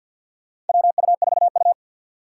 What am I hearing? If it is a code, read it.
Morse code, heard as a keyed tone, about 47 wpm, 700 Hz. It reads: WU4U
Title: USA Call Signs - 6x Character Spacing - 50wpm